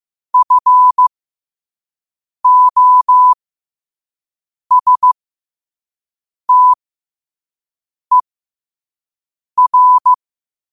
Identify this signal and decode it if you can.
Morse code, heard as a keyed tone, about 15 wpm, spaced as for 5 wpm, 1000 Hz. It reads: FOSTER
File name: 5wpm